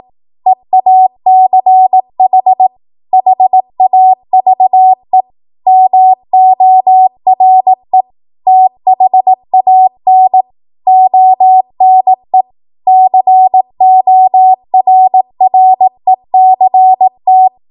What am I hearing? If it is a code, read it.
Morse code, heard as a keyed tone, about 18 wpm, 750 Hz. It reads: EACH HAVE MORE THAN ONE CORRECT